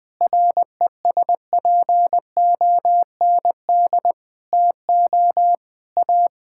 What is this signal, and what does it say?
Morse code, heard as a keyed tone, about 21 wpm, 700 Hz. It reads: RESPOND TO A
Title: Sets of 8 Words - 20wpm